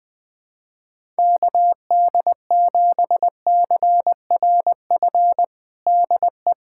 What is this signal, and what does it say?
Morse code, heard as a keyed tone, about 20 wpm, 700 Hz. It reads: KD7CRF DE
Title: USA Callsign DE Callsign - 6 Characters - 20wpm